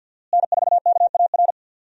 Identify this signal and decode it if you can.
Morse code, heard as a keyed tone, about 51 wpm, 700 Hz. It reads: N4KAR
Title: USA Call Signs - 5 Characters - Repeated 2x - 50wpm